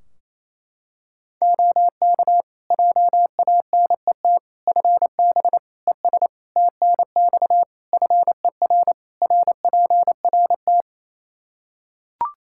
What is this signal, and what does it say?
Morse code, heard as a keyed tone, about 27 wpm, 700 Hz. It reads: OK JANET FB ES TNX FER RPRT
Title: Practice QSO - Exchange 2 - Info - 28wpm